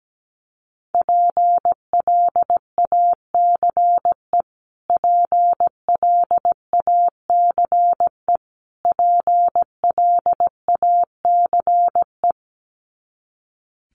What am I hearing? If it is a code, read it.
Morse code, heard as a keyed tone, about 17 wpm, 700 Hz. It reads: PLACE PLACE PLACE
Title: Top 500 Words Repeated 17wpm